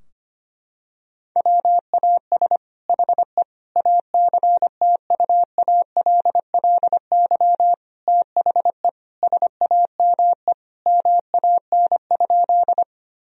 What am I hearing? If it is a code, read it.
Morse code, heard as a keyed tone, about 25 wpm, 700 Hz. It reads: WAS HE ACTUALLY THE SAME MAN?
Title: Sentences from Top 1000 Words - Review Part 4 - 25wpm